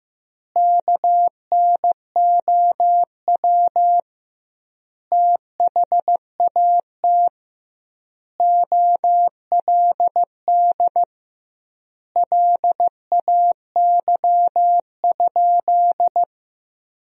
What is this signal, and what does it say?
Morse code, heard as a keyed tone, about 16 wpm, 700 Hz. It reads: KNOW THAT OLD LAY?
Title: 2x Word-Spacing - Sentences from Top 400 Words - 15wpm